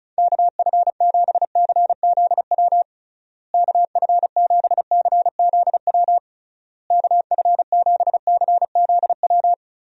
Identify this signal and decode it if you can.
Morse code, heard as a keyed tone, about 34 wpm, 700 Hz. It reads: KF7CZW KF7CZW KF7CZW
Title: USA Call Signs - 6 Characters - Repeated 3x - 35wpm